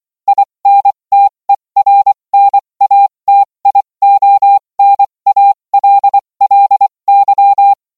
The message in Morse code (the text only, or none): INTERNATIONALLY